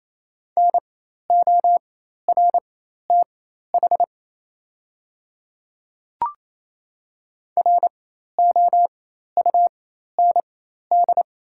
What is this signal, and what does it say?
Morse code, heard as a keyed tone, about 28 wpm, spaced as for 13 wpm, 700 Hz. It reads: NORTH ROUND
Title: 4x Character Spacing - Top 500 Words - 28wpm